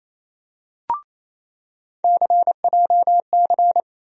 Morse code, heard as a keyed tone, about 28 wpm, 700 Hz. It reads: CJC
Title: USA Call Sign Postfixes - 3 Letters 28wpm